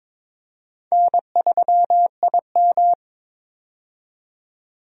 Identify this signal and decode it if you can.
Morse code, heard as a keyed tone, about 22 wpm, 700 Hz. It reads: N3IM
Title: USA Call Signs - 4 Characters - 22wpm